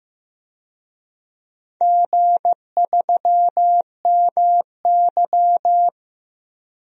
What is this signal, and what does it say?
Morse code, heard as a keyed tone, about 16 wpm, 700 Hz. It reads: G3MY
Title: DXCC-100 Call Signs - 4 Characters - 15wpm